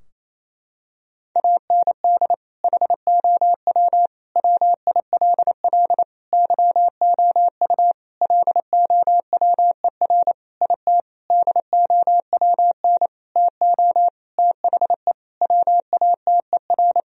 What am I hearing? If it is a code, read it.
Morse code, heard as a keyed tone, about 28 wpm, 700 Hz. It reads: AND HOW WILL YOU LOWER IT DOWN TO THE WATER
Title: Sentences from Top 1000 Words - Review Part 4 - 28wpm